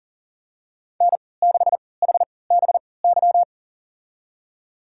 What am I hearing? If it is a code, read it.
Morse code, heard as a keyed tone, about 40 wpm, spaced as for 23 wpm, 700 Hz. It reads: N6HBY
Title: USA Call Signs - 3x Character Spacing - 40wpm